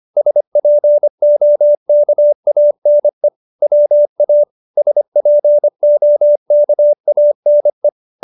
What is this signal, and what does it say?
Morse code, heard as a keyed tone, about 25 wpm, 590 Hz. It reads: SPOKANE WA SPOKANE
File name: QSO306 25